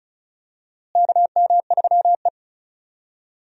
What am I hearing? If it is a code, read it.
Morse code, heard as a keyed tone, about 36 wpm, 700 Hz. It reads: KM3E
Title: USA Call Signs - 4 Characters - Repeated 3x - 35wpm